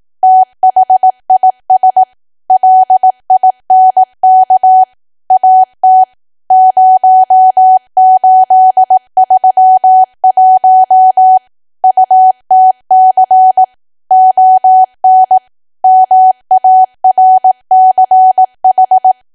Morse code, heard as a keyed tone, about 18 wpm, 750 Hz. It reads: THIS LINK AT 0831 UTC ON MARCH
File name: cwa / W1AW-Level2 / 18WPM